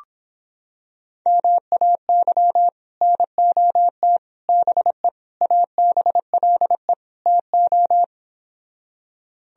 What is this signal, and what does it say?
Morse code, heard as a keyed tone, about 26 wpm, 700 Hz. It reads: MAY NOT BE ABLE TO